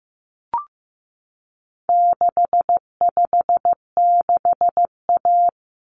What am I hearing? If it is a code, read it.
Morse code, heard as a keyed tone, about 15 wpm, 700 Hz. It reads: 656A
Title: Common Cut Numbers - 4-Digit 15wpm